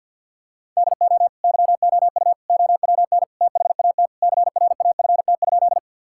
Morse code, heard as a keyed tone, about 50 wpm, 700 Hz. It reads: DO YOU OWN THAT CRAFT?
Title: Sentences from Top 1000 Words - Review Part 6 - 50wpm